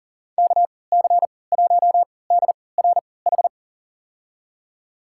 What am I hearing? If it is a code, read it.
Morse code, heard as a keyed tone, about 40 wpm, spaced as for 23 wpm, 700 Hz. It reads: KC1DRH
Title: USA Call Signs - 6 Characters - 3x Character Spacing - 40wpm